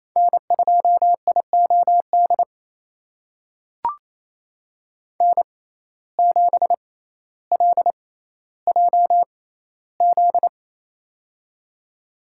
Morse code, heard as a keyed tone, about 28 wpm, spaced as for 14 wpm, 700 Hz. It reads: N2IOD N7LJZ
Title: USA Call Signs - 5 Characters - 6x Character Spacing - 28wpm